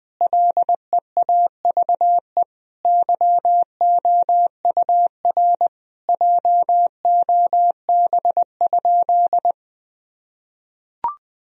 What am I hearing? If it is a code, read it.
Morse code, heard as a keyed tone, about 20 wpm, 700 Hz. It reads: LEAVE YOUR JOB?
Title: Sentences from Top 1000 Words - Review Part 3 - 20wpm